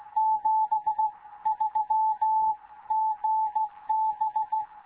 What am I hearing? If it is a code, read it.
Morse code, heard as a keyed tone, about 16 wpm, spaced as for 14 wpm, 870 Hz. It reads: 73 GB
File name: Morze-HA2MN / hangp / 2 11 GYAKORLAT